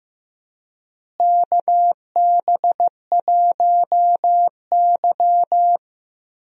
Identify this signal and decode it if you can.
Morse code, heard as a keyed tone, about 15 wpm, 700 Hz. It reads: KB1Y
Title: USA Call Signs - 4 Characters - Repeated 3x - 15wpm